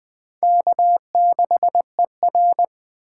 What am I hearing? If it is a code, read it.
Morse code, heard as a keyed tone, about 24 wpm, 700 Hz. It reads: K6ER
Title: Speed Racing - USA Call Signs 20wpm